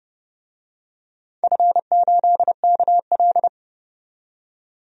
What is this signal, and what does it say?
Morse code, heard as a keyed tone, about 30 wpm, 700 Hz. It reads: F8KL
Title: DXCC-100 Call Signs - 4 Characters - 30wpm